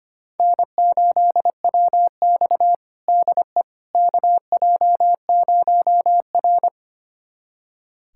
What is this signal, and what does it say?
Morse code, heard as a keyed tone, about 25 wpm, 700 Hz. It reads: N8WX DE KJ0R
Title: USA Callsign DE Callsign - 4 Characters - 25wpm